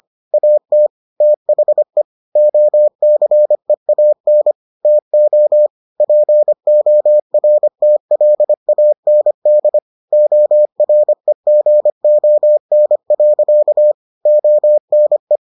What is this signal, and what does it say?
Morse code, heard as a keyed tone, about 25 wpm, 600 Hz. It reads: AT THE OCEAN TO PORTLAND OREGON. ONE